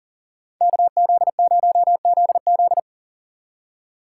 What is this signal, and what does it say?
Morse code, heard as a keyed tone, about 49 wpm, 700 Hz. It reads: KZ0ZZ
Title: Speed Racing - USA 2x2 Call Signs 40wpm